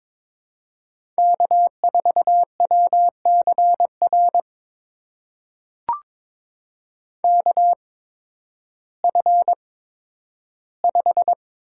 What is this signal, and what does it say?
Morse code, heard as a keyed tone, about 22 wpm, spaced as for 12 wpm, 700 Hz. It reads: K4WCR KF5
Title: USA Call Signs - 5 Characters - 8x Character Spacing - 22wpm